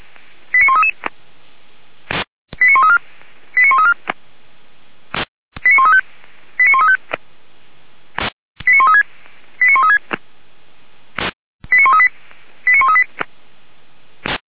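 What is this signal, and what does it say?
Five-tone selective call (ZVEI), heard as quick runs of five tones, tones of about 70 ms. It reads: 89133 89134 89134 89135 89135 89136 89136 89138 89138
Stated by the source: Five-tone ZVEI selective call received by SDR, logged by Feuerwehr Orlamünde.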